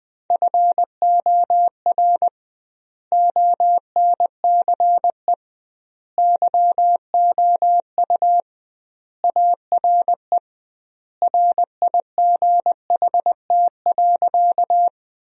Morse code, heard as a keyed tone, about 20 wpm, 700 Hz. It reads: FOR ONCE YOU ARE RIGHT.
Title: 2x Word-Spacing - Sentences from Top 300 Words - 20wpm